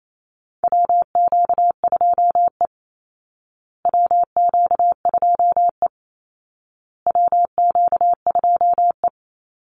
Morse code, heard as a keyed tone, about 28 wpm, 700 Hz. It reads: WQ2E WQ2E WQ2E
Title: Call Signs Repeated 2x - 28wpm